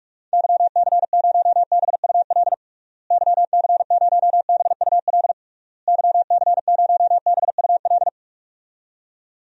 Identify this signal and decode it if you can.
Morse code, heard as a keyed tone, about 41 wpm, 700 Hz. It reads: YC0BUL YC0BUL YC0BUL
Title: DXCC-100 Call Signs - 6 Characters - Repeated 3x - 45wpm